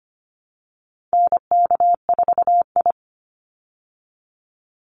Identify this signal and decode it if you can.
Morse code, heard as a keyed tone, about 25 wpm, 700 Hz. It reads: NK4I
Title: Call Signs Repeated 2x - 25wpm